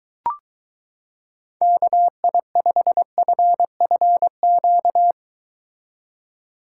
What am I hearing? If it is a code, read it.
Morse code, heard as a keyed tone, about 23 wpm, 700 Hz. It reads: KI5FFQ